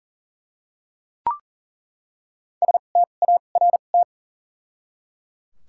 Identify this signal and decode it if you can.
Morse code, heard as a keyed tone, about 40 wpm, spaced as for 29 wpm, 700 Hz. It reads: START
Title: 2x Character Spacing - Top 300 Words - 40wpm